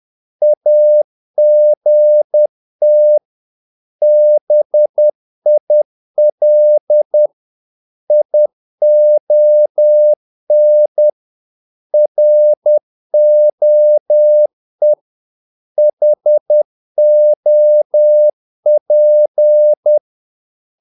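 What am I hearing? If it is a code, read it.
Morse code, heard as a keyed tone, about 10 wpm, 600 Hz. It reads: AGT BIL ION ROE HOP
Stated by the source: Morse practice of short Danish words at 10 wpm, from CW med Gnister.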